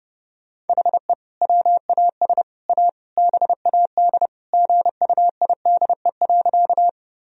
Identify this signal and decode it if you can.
Morse code, heard as a keyed tone, about 29 wpm, 700 Hz. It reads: HE WAS A BAD GUIDE.
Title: Sentences from Top 1000 Words - Review Part 3 - 30wpm